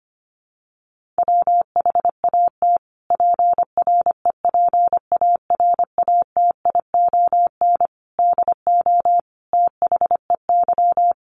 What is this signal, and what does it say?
Morse code, heard as a keyed tone, about 25 wpm, 700 Hz. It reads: WHAT PREPARATION DO THEY
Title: Sentences from Top 1400 Words 25wpm